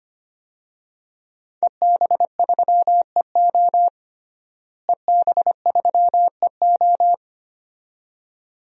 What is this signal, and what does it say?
Morse code, heard as a keyed tone, about 25 wpm, 700 Hz. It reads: EB3EO EB3EO
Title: DXCC-100 Call Signs - 5 Characters - Repeated 2x - 25wpm